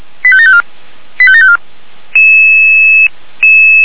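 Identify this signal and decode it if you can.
Five-tone selective call (ZVEI), heard as quick runs of five tones, tones of about 70 ms. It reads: 85754 85754
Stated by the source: ZVEI five-tone call 85381, received on an SDR.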